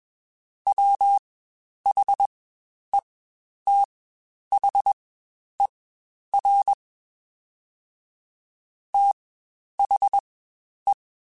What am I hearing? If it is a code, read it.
Morse code, heard as a keyed tone, about 21 wpm, spaced as for 10 wpm, 800 Hz. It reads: WHETHER THE
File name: Spaced / Chuckles Spaced-mp3s / HalfFull sp @20